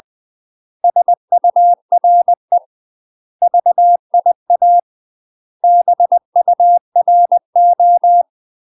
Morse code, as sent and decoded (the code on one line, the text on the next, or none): ... ..- .-. / ...- .. .- / -... ..- .-. ---
SURE VIA BURO